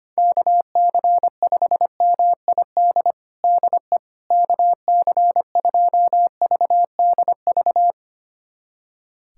Morse code, heard as a keyed tone, about 25 wpm, 700 Hz. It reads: KC5MID DE KC2VDV
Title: USA Callsign DE Callsign - 6 Characters - 25wpm